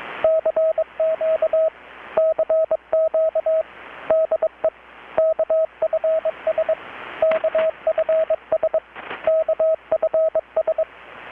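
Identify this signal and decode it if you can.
Morse code, heard as a keyed tone, about 21 wpm, 640 Hz, with radio noise. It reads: CQ CQ DE KFS KFS KFS
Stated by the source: Morse from KPH's Night of Nights XX maritime radiotelegraphy event, received in Chiapas, Mexico about 3,800 km away on a Kenwood R-600 with a grounded 10 m long-wire antenna.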